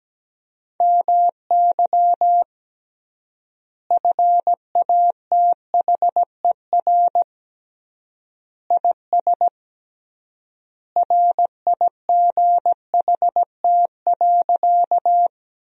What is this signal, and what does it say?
Morse code, heard as a keyed tone, about 17 wpm, 700 Hz. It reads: MY FATHER IS RIGHT.
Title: 3x Word-Spacing - Sentences from Top 300 Words - 17wpm